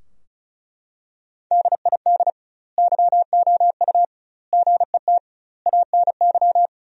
Morse code, heard as a keyed tone, about 34 wpm, 700 Hz. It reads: DID YOU GET ANY
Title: 2x Word-Spacing - Sentences from Top 400 Words - 35wpm